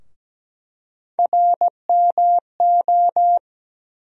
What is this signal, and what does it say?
Morse code, heard as a keyed tone, about 17 wpm, 700 Hz. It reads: RMO
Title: USA Call Sign Postfixes - 3 Letters 17wpm